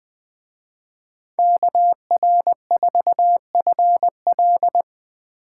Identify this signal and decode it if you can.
Morse code, heard as a keyed tone, about 24 wpm, 700 Hz. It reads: KR4FL
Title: Speed Racing - USA 2x2 Call Signs 20wpm